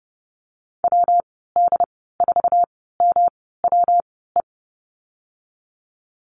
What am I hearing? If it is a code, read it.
Morse code, heard as a keyed tone, about 30 wpm, spaced as for 17 wpm, 700 Hz. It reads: WD4MWE
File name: Lesson 19g - Letter w - Call Signs 30wpm